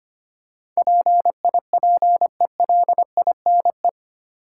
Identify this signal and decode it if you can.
Morse code, heard as a keyed tone, about 25 wpm, 700 Hz. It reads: PIPELINE
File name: 8 Letter Words - First 3 Letters Given - 25wpm